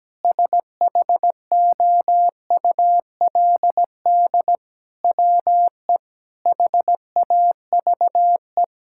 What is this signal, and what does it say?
Morse code, heard as a keyed tone, about 17 wpm, 700 Hz. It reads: SHOULD WE HAVE
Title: Sentences from Top 1000 Words - Review Part 5 - 17wpm